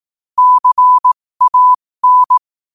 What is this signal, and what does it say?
Morse code, heard as a keyed tone, about 18 wpm, spaced as for 15 wpm, 1000 Hz. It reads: CAN
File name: qotdfiles / 15wpm